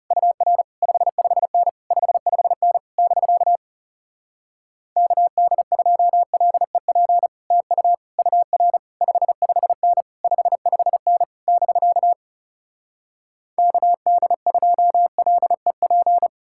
Morse code, heard as a keyed tone, about 40 wpm, 700 Hz. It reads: UR 55N 55N <BK> KD2LEP TU UR 55N 55N <BK> KD2LEP